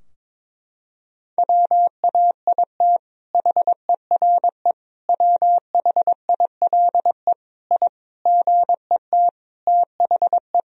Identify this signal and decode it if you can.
Morse code, heard as a keyed tone, about 22 wpm, 700 Hz. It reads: WAIT HERE WHILE I GET THE
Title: Sentences from Top 1000 Words - Review Part 4 - 22wpm